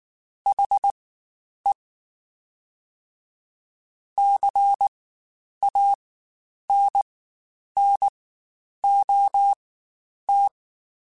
Morse code, heard as a keyed tone, about 19 wpm, spaced as for 9 wpm, 800 Hz. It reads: HE CANNOT
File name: Spaced / Chuckles Spaced-mp3s / HalfFull sp @18